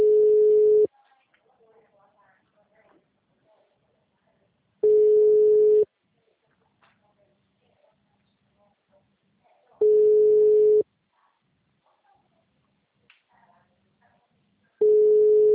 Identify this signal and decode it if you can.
Ringback tone (Europe, 425 Hz), with speech.